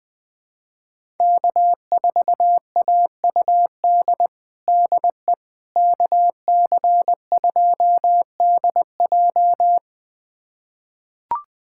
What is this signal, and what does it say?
Morse code, heard as a keyed tone, about 20 wpm, 700 Hz. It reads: K4AUD DE KC2DJ
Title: USA Callsign DE Callsign - 5 Characters - 20wpm